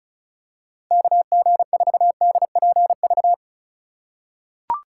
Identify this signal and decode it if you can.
Morse code, heard as a keyed tone, about 34 wpm, 700 Hz. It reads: KG4DPV
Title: USA Call Signs - 6 Characters - 2x Character Spacing - 35wpm